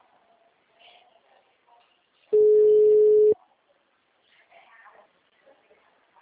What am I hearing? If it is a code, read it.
Ringback tone (Europe, 425 Hz).